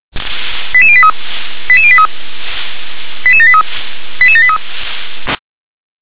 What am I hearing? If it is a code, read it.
Five-tone selective call (ZVEI), heard as quick runs of five tones, tones of about 70 ms. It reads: 88083 88083 88783 88783